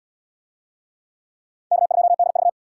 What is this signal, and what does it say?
Morse code, heard as a keyed tone, about 70 wpm, 700 Hz. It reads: K2DV